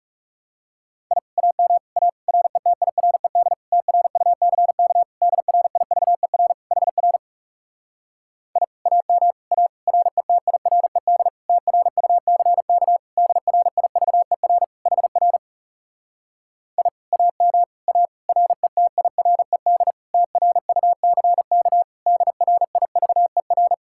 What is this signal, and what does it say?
Morse code, heard as a keyed tone, about 40 wpm, 700 Hz. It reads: I AM A RETIRED TRUCK DRIVER HR I AM A RETIRED TRUCK DRIVER HR I AM A RETIRED TRUCK DRIVER